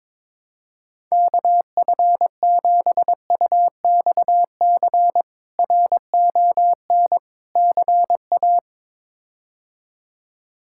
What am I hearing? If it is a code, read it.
Morse code, heard as a keyed tone, about 21 wpm, 700 Hz. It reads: KF7UXC RON CA
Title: K1USN SST QSO Practice - 22wpm